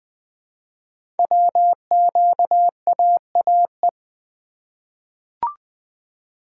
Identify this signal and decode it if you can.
Morse code, heard as a keyed tone, about 20 wpm, 700 Hz. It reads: WQAAE
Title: Code Groups - 5 Letter:Numbers - 20wpm